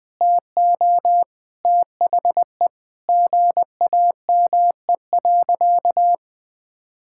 Morse code, heard as a keyed tone, about 21 wpm, 700 Hz. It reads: TO THE GAME.
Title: Sentences from Top 1000 Words - Review Part 6 - 20wpm